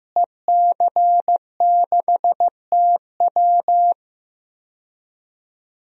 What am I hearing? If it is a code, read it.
Morse code, heard as a keyed tone, about 15 wpm, 700 Hz. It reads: EC6TW
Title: DXCC-100 Call Signs - 5 Characters - Repeated 3x - 15wpm